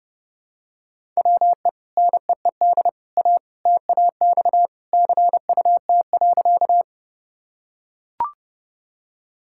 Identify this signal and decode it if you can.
Morse code, heard as a keyed tone, about 32 wpm, 700 Hz. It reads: WE NEED A TAX CUT.
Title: Sentences from Top 1000 Words - Review Part 5 - 30wpm